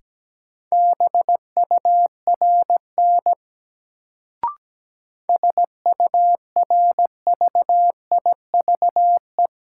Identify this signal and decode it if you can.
Morse code, heard as a keyed tone, about 17 wpm, 700 Hz. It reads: BURN SURVIVE
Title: Sentences from Top 1300 Words - New Word Drill - 17wpm